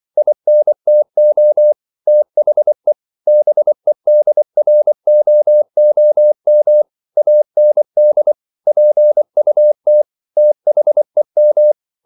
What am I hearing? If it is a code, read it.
Morse code, heard as a keyed tone, about 24 wpm, 600 Hz. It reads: INTO THE BEDROOM AND PUT THEM